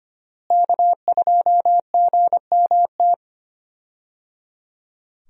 Morse code, heard as a keyed tone, about 25 wpm, 700 Hz. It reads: K2GMT
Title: USA Call Signs - 5 Characters - Repeated 3x - 25wpm